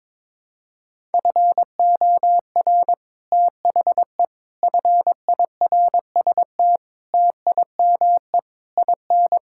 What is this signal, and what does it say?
Morse code, heard as a keyed tone, about 22 wpm, 700 Hz. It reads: FOR THE FIRST TIME IN
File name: Sets of 9 Words - 22wpm